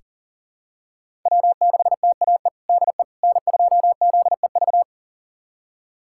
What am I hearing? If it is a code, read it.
Morse code, heard as a keyed tone, about 40 wpm, 700 Hz. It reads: W6TAE DE N2ZEV
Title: USA Callsign DE Callsign - 5 Characters - 40wpm